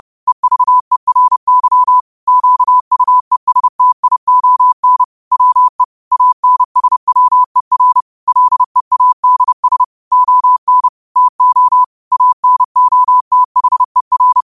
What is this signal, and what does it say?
Morse code, heard as a keyed tone, about 30 wpm, 1000 Hz. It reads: EVERY QUESTION WE ANSWER LEADS ON TO ANOTHER